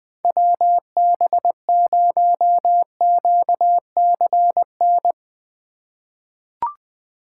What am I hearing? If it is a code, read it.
Morse code, heard as a keyed tone, about 20 wpm, 700 Hz. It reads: WB0QCN
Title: USA Call Signs - 6 Characters - 2x Character Spacing - 20wpm